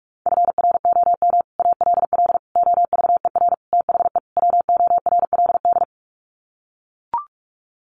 Morse code, heard as a keyed tone, about 44 wpm, 700 Hz. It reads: FROM ALL OVER THE WORLD
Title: Sets of 5 Words 45wpm